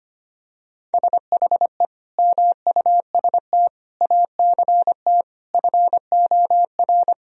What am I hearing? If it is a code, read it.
Morse code, heard as a keyed tone, about 25 wpm, 700 Hz. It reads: SHE MUST ACT FOR
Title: Sentences from Top 1000 Words - Review Part 1 - 25wpm